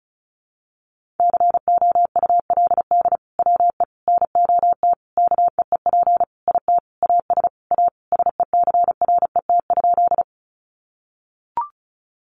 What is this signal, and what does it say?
Morse code, heard as a keyed tone, about 34 wpm, 700 Hz. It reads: COULD WE NOT KEEP IT AS A SECRET?
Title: Sentences from Top 1200 Words 35wpm